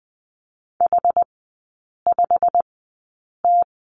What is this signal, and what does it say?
Morse code, heard as a keyed tone, about 20 wpm, spaced as for 8 wpm, 700 Hz. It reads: H5T
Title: 191 Lesson 31e - Letter v - Three Characters 20wpm